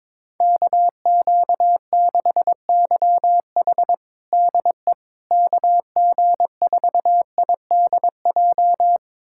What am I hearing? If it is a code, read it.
Morse code, heard as a keyed tone, about 22 wpm, 700 Hz. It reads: KQ6YH DE KG4IDJ
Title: USA Callsign DE Callsign 22wpm